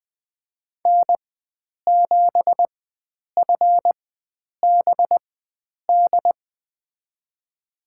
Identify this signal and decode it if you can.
Morse code, heard as a keyed tone, about 21 wpm, spaced as for 9 wpm, 700 Hz. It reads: N7FBD
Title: USA Call Signs - 5 Characters - 4x Character Spacing - 20wpm